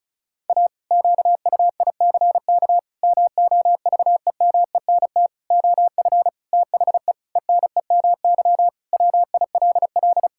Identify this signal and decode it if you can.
Morse code, heard as a keyed tone, about 34 wpm, 700 Hz. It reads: A QUICK MOVEMENT OF THE ENEMY WILL
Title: Pangram Challenge v2 - 35wpm